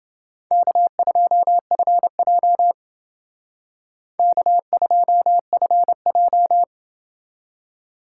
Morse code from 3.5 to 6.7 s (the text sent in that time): K2FJ